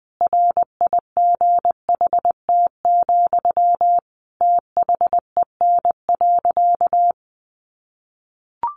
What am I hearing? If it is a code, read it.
Morse code, heard as a keyed tone, about 20 wpm, 700 Hz. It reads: RIGHT, THEN.
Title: Sentences from Top 200 Words 20wpm